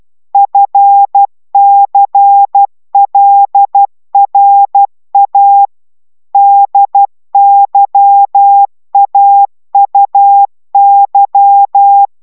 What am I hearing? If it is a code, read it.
Morse code, heard as a keyed tone, about 12 wpm, 800 Hz. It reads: FCLRA DYAUY
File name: taviro tanulas / cw 69 19 10